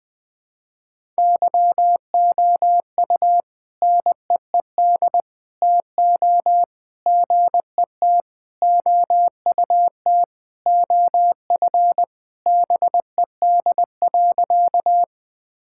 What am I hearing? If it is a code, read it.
Morse code, heard as a keyed tone, about 20 wpm, 700 Hz. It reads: YOU NEED TO GET OUT OF BED.
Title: Sentences with 3-Word Phrases - 20wpm